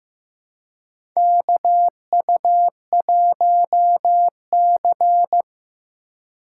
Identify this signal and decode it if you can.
Morse code, heard as a keyed tone, about 15 wpm, 700 Hz. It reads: KU1C
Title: USA Call Signs - 4x Character Spacing - 15wpm